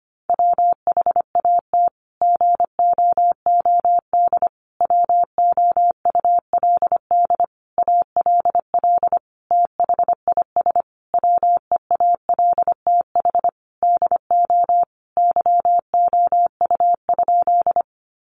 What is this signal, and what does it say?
Morse code, heard as a keyed tone, about 25 wpm, 700 Hz. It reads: WHAT GOOD WOULD ALL THIS WEALTH DO YOU?
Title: Sentences from Top 1200 Words 25wpm